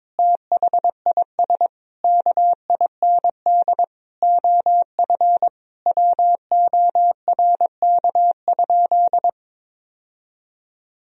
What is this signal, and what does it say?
Morse code, heard as a keyed tone, about 21 wpm, 700 Hz. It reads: THIS KIND OF WORK?
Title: Sentences from Top 1000 Words - Review Part 3 - 22wpm